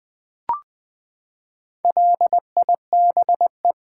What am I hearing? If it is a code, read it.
Morse code, heard as a keyed tone, about 20 wpm, 700 Hz. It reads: LIBE
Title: Build A Word - 4 Letters At A Time - 10-Letter Words - 20wpm